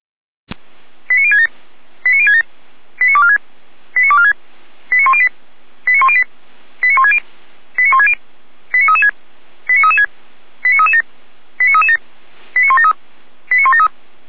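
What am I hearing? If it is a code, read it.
Five-tone selective call (ZVEI), heard as quick runs of five tones, tones of about 70 ms. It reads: 89067 89067 89246 89246 89108 89108 89160 89160 89336 89336 89337 89337 89173 89173
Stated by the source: ZVEI five-tone selective call 89560 received on an SDR.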